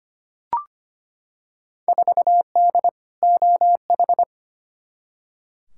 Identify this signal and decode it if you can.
Morse code, heard as a keyed tone, about 25 wpm, 700 Hz. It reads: 4D OH
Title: Field Day Exchanges Encore - 25wpm